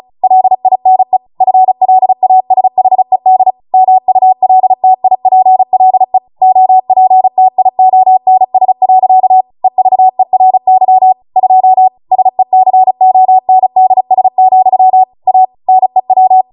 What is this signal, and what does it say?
Morse code, heard as a keyed tone, about 36 wpm, 750 Hz. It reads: LINE FLASHED MULTIPLE OPTIONS. EVERY 2 SECONDS, A NEW